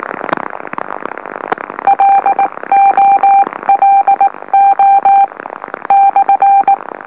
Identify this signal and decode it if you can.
Morse code, heard as a keyed tone, about 18 wpm, 790 Hz. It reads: LOLO /
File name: M04- -2345utc-LOLO